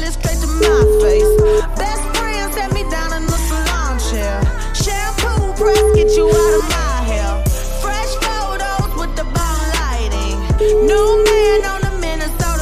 Ringback tone (Europe, 425 Hz), with noise.